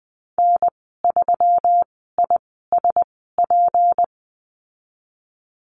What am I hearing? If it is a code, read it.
Morse code, heard as a keyed tone, about 19 wpm, spaced as for 14 wpm, 700 Hz. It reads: N3ISP
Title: 173 Lesson 28g - Number 9 - Call Signs 20wpm